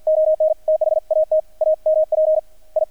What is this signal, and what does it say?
Morse code, heard as a keyed tone, about 48 wpm, 630 Hz, with radio noise. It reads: ON THAT AMP I
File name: archived / CRSnet